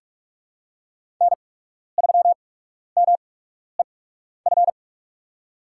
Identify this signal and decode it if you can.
Morse code, heard as a keyed tone, about 47 wpm, spaced as for 12 wpm, 700 Hz. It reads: N3MEF
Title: USA Call Signs - 8x Character Spacing - 45wpm